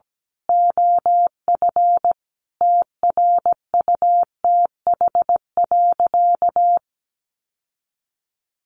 Morse code, heard as a keyed tone, about 17 wpm, 700 Hz. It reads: OF TRUTH.
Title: Sentences from Top 1100 Words 17wpm